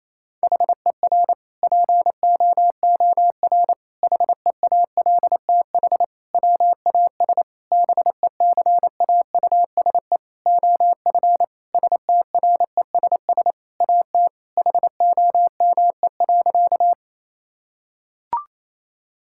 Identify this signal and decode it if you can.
Morse code, heard as a keyed tone, about 28 wpm, 700 Hz. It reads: HER POOR HEALTH WAS BECAUSE OF STRESS AT HOME.
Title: Sentences from Top 1000 Words - Review Part 6 - 28wpm